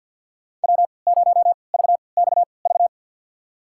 Morse code, heard as a keyed tone, about 52 wpm, spaced as for 29 wpm, 700 Hz. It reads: W0VXV
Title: USA Call Signs - 5 Characters - 3x Character Spacing - 50wpm